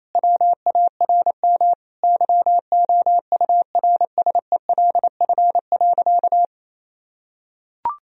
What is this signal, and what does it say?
Morse code, heard as a keyed tone, about 28 wpm, 700 Hz. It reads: WARM YOURSELF.
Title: Sentences from Top 1000 Words - Review Part 2 - 28wpm